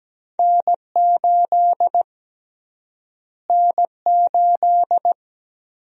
Morse code, heard as a keyed tone, about 17 wpm, 700 Hz. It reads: N8 N8
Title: USA Call Sign Prefixes - Repeated 2x - 17wpm